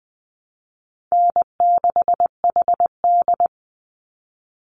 Morse code, heard as a keyed tone, about 20 wpm, 700 Hz. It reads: N6HD
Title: Easy Call Signs Repeated 3x - 20wpm